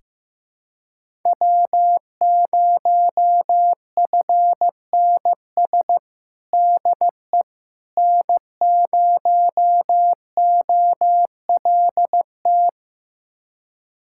Morse code, heard as a keyed tone, about 15 wpm, 700 Hz. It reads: W0FNS DE N0OLT
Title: USA Callsign DE Callsign - 5 Characters - 15wpm